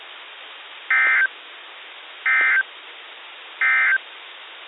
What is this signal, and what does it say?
Emergency Alert System (SAME) tones, heard as harsh buzzing bursts. It is heard from 0.8 to 4.1 s.
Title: KHB37 NOAA Weather Radio 162.55 Norfolk, VA